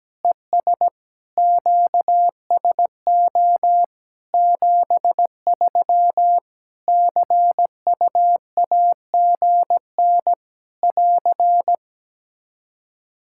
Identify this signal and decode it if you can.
Morse code, heard as a keyed tone, about 17 wpm, 700 Hz. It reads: ES QSO 73 CUAGN <AR>
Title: Practice QSO - Exchange 4 - Ending - 17wpm